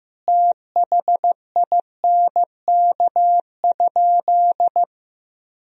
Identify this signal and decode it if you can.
Morse code, heard as a keyed tone, about 16 wpm, 700 Hz. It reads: THINK?
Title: Sentences from Top 200 Words and Abbreviations 15wpm